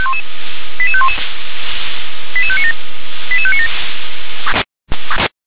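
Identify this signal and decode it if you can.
Five-tone selective call (ZVEI), heard as quick runs of five tones, tones of about 70 ms. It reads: 88511 88597 88597
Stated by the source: Five-tone ZVEI selective call received on SDR.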